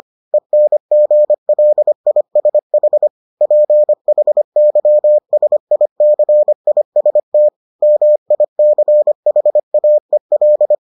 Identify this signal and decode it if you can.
Morse code, heard as a keyed tone, about 25 wpm, 600 Hz. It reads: ENGLISH PHYSICIST MICHAEL